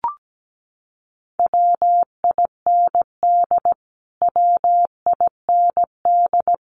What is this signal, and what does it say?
Morse code, heard as a keyed tone, about 17 wpm, 700 Hz. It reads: WIND WIND
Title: Top 500 Words Repeated 17wpm